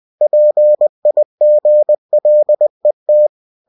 Morse code, heard as a keyed tone, about 20 wpm, 600 Hz. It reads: PIGLET
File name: cw-books / winnie-the-pooh / winnie-20wpm-page-0101